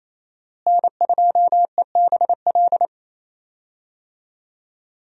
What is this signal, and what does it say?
Morse code, heard as a keyed tone, about 28 wpm, 700 Hz. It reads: N2EBL
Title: USA Call Signs - 5 Characters - Repeated 3x - 28wpm